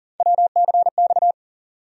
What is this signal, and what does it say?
Morse code, heard as a keyed tone, about 37 wpm, 700 Hz. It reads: WCX
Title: USA Call Sign Postfixes - 3 Letters 40wpm